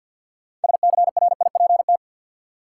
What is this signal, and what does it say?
Morse code, heard as a keyed tone, about 53 wpm, 700 Hz. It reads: SCRIPT
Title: Six Letter Words - 50wpm